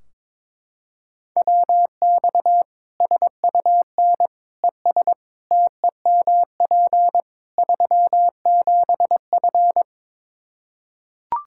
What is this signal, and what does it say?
Morse code, heard as a keyed tone, about 22 wpm, 700 Hz. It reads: WX SUN ES TEMP 37F
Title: Practice QSO - Exchange 2 - Info - 22wpm